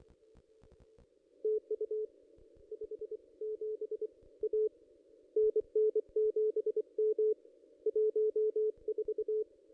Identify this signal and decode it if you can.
Morse code, heard as a keyed tone, about 27 wpm, 430 Hz. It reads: TU 57 A NN7M 14
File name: contests / ss-cw-k5tr-op-n5rz -1600